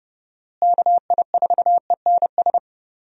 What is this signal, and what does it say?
Morse code, heard as a keyed tone, about 30 wpm, 700 Hz. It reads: KI4ENS
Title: USA Call Signs - 6 Characters - Repeated 3x - 30wpm